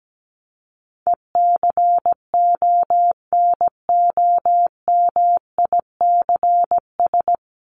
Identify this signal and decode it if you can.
Morse code, heard as a keyed tone, about 17 wpm, 700 Hz. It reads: ECONOMICS
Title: Top 501-750 Words 17wpm